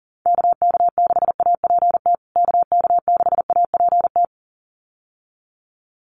Morse code, heard as a keyed tone, about 40 wpm, 700 Hz. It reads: KK6APT KK6APT
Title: USA Call Signs Repeated 2x 40wpm